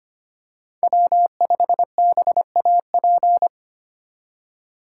Morse code, heard as a keyed tone, about 25 wpm, 700 Hz. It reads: W5BAP
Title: USA Call Signs - 5 Characters - 3x Character Spacing - 25wpm